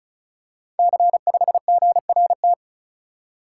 Morse code, heard as a keyed tone, about 36 wpm, 700 Hz. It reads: C5GRT
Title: Code Groups - 5 Letter:Numbers - 35wpm